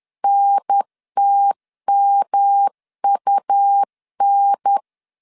Morse code, heard as a keyed tone, about 10 wpm, 800 Hz. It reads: NTMUN
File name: 10wpm